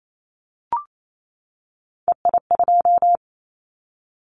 Morse code, heard as a keyed tone, about 28 wpm, 700 Hz. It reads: EI2
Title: Callsign Prefixes for Top 50 DXCC - 28wpm